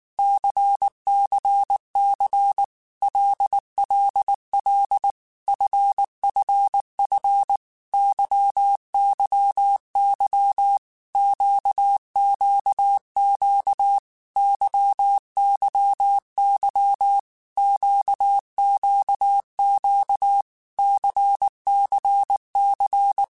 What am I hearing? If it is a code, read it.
Morse code, heard as a keyed tone, about 19 wpm, 800 Hz. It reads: CCC LLL FFF YYY QQQ YYY QQQ CCC
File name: Groups@18 / Groups@18 mp3 / LFCYQ@18